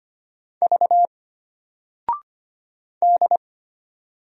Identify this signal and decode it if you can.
Morse code, heard as a keyed tone, about 25 wpm, 700 Hz. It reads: V D
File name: Single-Letter-Number-25wpm